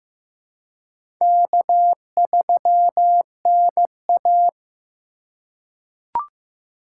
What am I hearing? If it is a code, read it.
Morse code, heard as a keyed tone, about 15 wpm, 700 Hz. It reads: K3NA
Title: USA Call Signs - 4 Characters - 2x Character Spacing - 15wpm